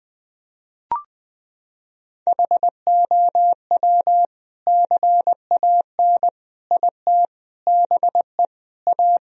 Morse code, heard as a keyed tone, about 21 wpm, 700 Hz. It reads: HOW CAN IT BE A
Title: Sentences from Top 1000 Words - Review Part 2 - 20wpm